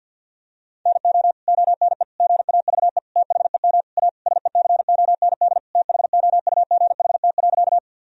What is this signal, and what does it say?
Morse code, heard as a keyed tone, about 52 wpm, 700 Hz. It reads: NO ONE GAVE THEM A SECOND THOUGHT.
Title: Sentences from Top 1000 Words - Review Part 1 - 50wpm